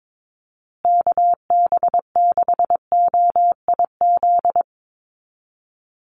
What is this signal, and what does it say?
Morse code, heard as a keyed tone, about 22 wpm, 700 Hz. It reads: KB6OIZ
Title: USA Call Signs Repeated 2x 22wpm